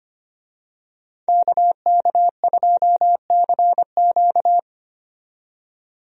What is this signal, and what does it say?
Morse code, heard as a keyed tone, about 25 wpm, 700 Hz. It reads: KK2CQ
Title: USA Call Signs - 5 Characters - 25wpm